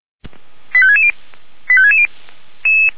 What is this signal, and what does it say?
Five-tone selective call (ZVEI), heard as quick runs of five tones, tones of about 70 ms. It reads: 85660 85660